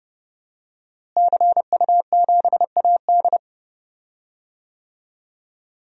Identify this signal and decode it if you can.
Morse code, heard as a keyed tone, about 30 wpm, 700 Hz. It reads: CU7AD
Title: DXCC-100 Call Signs - 5 Characters - Repeated 3x - 30wpm